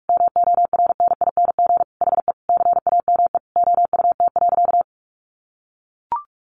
Morse code, heard as a keyed tone, about 44 wpm, 700 Hz. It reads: MORNING HE CAME OUT.
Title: Sentences from Top 1300 Words 45wpm